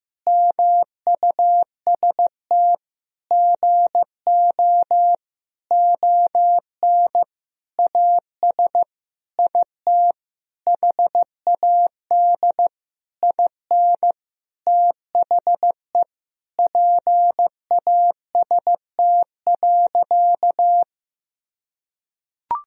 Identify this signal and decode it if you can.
Morse code, heard as a keyed tone, about 15 wpm, 700 Hz. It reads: MUST GO ON AS IT HAD IN THE PAST.
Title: Sentences from Top 1000 Words - Review Part 3 - 15wpm